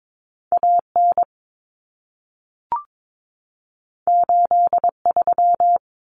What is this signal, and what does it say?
Morse code, heard as a keyed tone, about 22 wpm, 700 Hz. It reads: AN 83
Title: Common Cut Numbers - 2-Digit 22wpm